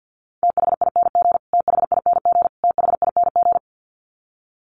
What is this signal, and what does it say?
Morse code, heard as a keyed tone, about 50 wpm, 700 Hz. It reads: THING THING THING
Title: Top 500 Words Repeated 50wpm